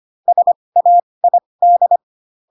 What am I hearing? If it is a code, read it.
Morse code, heard as a keyed tone, about 25 wpm, spaced as for 20 wpm, 700 Hz. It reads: SAID
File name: Pinocchio / Pinocchio ch4 15 25 ik1hge